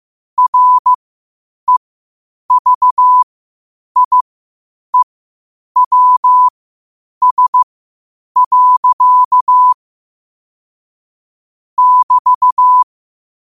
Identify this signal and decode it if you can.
Morse code, heard as a keyed tone, about 15 wpm, spaced as for 8 wpm, 1000 Hz. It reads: REVIEWS. <BT>
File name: qotdfiles / 8wpm